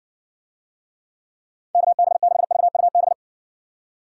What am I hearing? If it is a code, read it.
Morse code, heard as a keyed tone, about 55 wpm, 700 Hz. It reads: KB6VUB